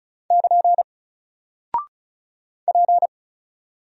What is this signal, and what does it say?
Morse code, heard as a keyed tone, about 33 wpm, 700 Hz. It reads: <KN> P
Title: Single Character (Letter:Number:Punctuation:Prosign) 35wpm